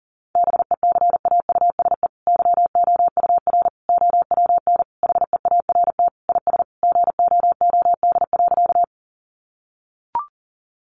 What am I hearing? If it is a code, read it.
Morse code, heard as a keyed tone, about 37 wpm, 700 Hz. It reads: BECAUSE YOUR OWN HEART IS GOOD.
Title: Sentences from Top 1200 Words 40wpm